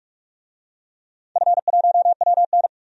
Morse code, heard as a keyed tone, about 45 wpm, 700 Hz. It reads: F1WN
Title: DXCC-100 Call Signs - 4 Characters - 45wpm